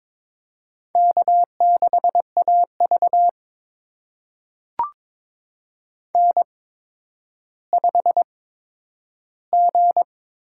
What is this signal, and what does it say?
Morse code, heard as a keyed tone, about 22 wpm, spaced as for 11 wpm, 700 Hz. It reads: K6AV N5G
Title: USA Call Signs - 4 Characters - 8x Character Spacing - 22wpm